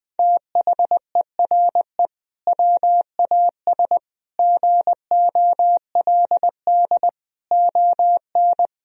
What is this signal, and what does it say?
Morse code, heard as a keyed tone, about 20 wpm, 700 Hz. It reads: THERE WAS GOLD ON
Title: Sentences from Top 1000 Words - Review Part 4 - 20wpm